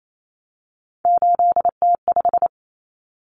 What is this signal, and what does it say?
Morse code, heard as a keyed tone, about 28 wpm, 700 Hz. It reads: 8T5
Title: Common Cut Numbers - 3-Digit 28wpm